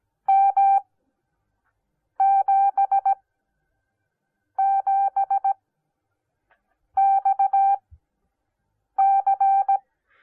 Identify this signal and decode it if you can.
Morse code, heard as a keyed tone, about 17 wpm, spaced as for 5 wpm, 780 Hz. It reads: M77XC